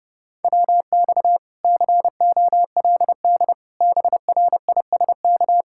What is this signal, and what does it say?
Morse code, heard as a keyed tone, about 33 wpm, 700 Hz. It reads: WX COLD BRISK